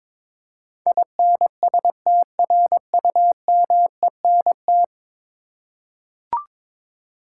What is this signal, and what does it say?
Morse code, heard as a keyed tone, about 22 wpm, 700 Hz. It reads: INSTRUMENT
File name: practice / 22wpm / Ten Letter Words - 22wpm